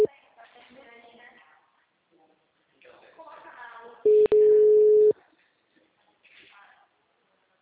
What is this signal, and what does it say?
Ringback tone (Europe, 425 Hz), with speech.